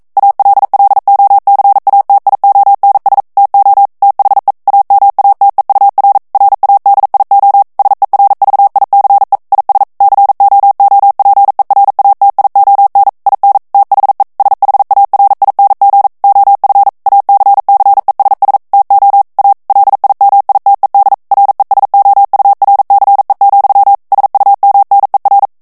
Morse code, heard as a keyed tone, about 42 wpm, 800 Hz. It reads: ALLOCATIONS TO THE AMATEUR RADIO SERVICE IS COOPERATION IN THE SHARING OF ACCESS TO A LIMITED RESOURCE, SUMNER